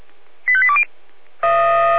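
Five-tone selective call (ZVEI), heard as quick runs of five tones, tones of about 70 ms. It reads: 85820
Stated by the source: ZVEI five-tone selective call received by SDR.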